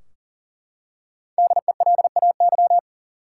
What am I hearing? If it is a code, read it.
Morse code, heard as a keyed tone, about 40 wpm, 700 Hz. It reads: DELAY
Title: Five Letter Words - 40wpm